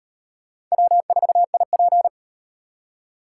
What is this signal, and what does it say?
Morse code, heard as a keyed tone, about 38 wpm, 700 Hz. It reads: W4IP